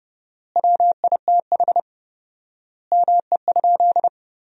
Morse code, heard as a keyed tone, about 30 wpm, 700 Hz. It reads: WITH ME?
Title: 4x Word-Spacing - Sentences from Top 300 Words - 30wpm